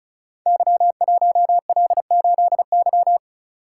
Morse code, heard as a keyed tone, about 36 wpm, 700 Hz. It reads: Y1L8Y
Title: Code Groups - 5 Letter:Numbers - 35wpm